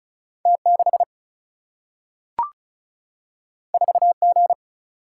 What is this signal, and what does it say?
Morse code, heard as a keyed tone, about 36 wpm, 700 Hz. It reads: T6 4G
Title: Single-Letter Single-Number 35wpm